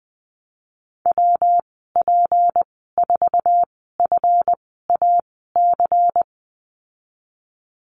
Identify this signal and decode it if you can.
Morse code, heard as a keyed tone, about 19 wpm, spaced as for 14 wpm, 700 Hz. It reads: WP4FAC
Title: 173 Lesson 28g - Number 9 - Call Signs 20wpm